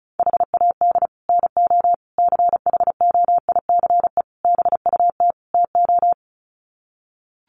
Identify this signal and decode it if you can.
Morse code, heard as a keyed tone, about 34 wpm, 700 Hz. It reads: HAD NO CHOICE BUT TO
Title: Sets of 5 Words 35wpm